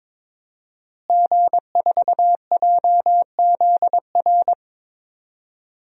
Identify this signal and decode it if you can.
Morse code, heard as a keyed tone, about 22 wpm, 700 Hz. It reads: G4JZR
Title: DXCC-100 Call Signs - 5 Characters - Repeated 3x - 22wpm